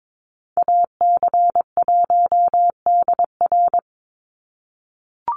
Morse code, heard as a keyed tone, about 22 wpm, 700 Hz. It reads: AC1DR
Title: USA Call Signs 22wpm